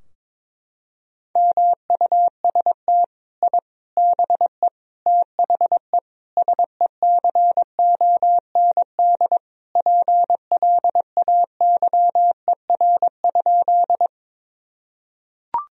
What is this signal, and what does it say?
Morse code, heard as a keyed tone, about 22 wpm, 700 Hz. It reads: MUST I BE THE SECOND PLAYER?
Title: Sentences from Top 1000 Words - Review Part 4 - 22wpm